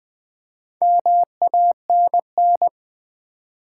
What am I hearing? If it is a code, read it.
Morse code, heard as a keyed tone, about 20 wpm, 690 Hz. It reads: MANN
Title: Build A Word - 2 Letters At A Time - 6-Letter Words - 20wpm